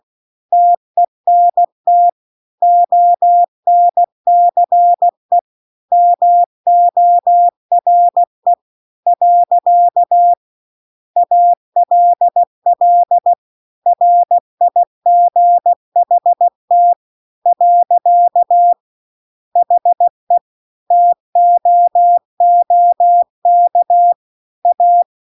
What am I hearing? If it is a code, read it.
Morse code, heard as a keyed tone, about 16 wpm, 700 Hz. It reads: TENT ONCE MORE . ALL RIGHT . HE TOOK A